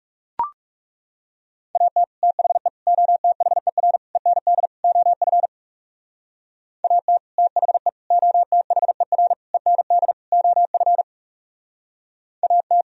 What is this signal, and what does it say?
Morse code, heard as a keyed tone, about 40 wpm, 700 Hz. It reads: AT THE OTHER END OF AT THE OTHER END OF AT